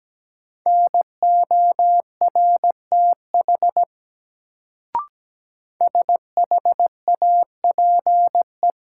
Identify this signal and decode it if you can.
Morse code, heard as a keyed tone, about 17 wpm, 700 Hz. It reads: NORTH SHAPE
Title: Sets of 5 Words Encore - New Word Drill - 17wpm